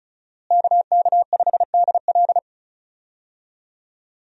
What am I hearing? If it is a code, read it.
Morse code, heard as a keyed tone, about 36 wpm, 700 Hz. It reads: KK5DL
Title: USA Call Signs - 5 Characters - Repeated 3x - 35wpm